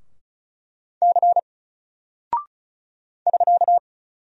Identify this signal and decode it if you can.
Morse code, heard as a keyed tone, about 36 wpm, 700 Hz. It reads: C <SK>
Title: Single Character (Letter:Number:Punctuation:Prosign) 35wpm